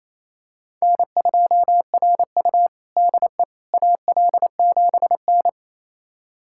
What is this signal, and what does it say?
Morse code, heard as a keyed tone, about 28 wpm, 700 Hz. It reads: N2RU DE AL7N
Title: USA Callsign DE Callsign - 4 Characters - 28wpm